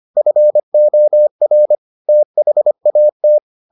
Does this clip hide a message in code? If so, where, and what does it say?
Yes: Morse code from 0.2 to 3.4 s, heard as a keyed tone, about 25 wpm, 600 Hz. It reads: FOR THAT